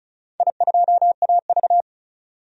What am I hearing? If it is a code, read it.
Morse code, heard as a keyed tone, about 36 wpm, 700 Hz. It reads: I2AV
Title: DXCC-100 Call Signs - 4 Characters - 35wpm